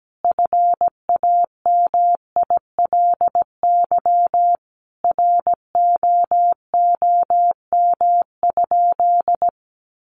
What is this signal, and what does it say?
Morse code, heard as a keyed tone, about 17 wpm, 700 Hz. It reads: FAMILY ROOM?
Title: Sentences from Top 1200 Words 17wpm